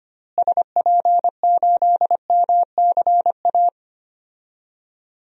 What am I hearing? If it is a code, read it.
Morse code, heard as a keyed tone, about 25 wpm, 700 Hz. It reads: SP8MCA
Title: DXCC-100 Call Signs - 6 Characters - Repeated 3x - 25wpm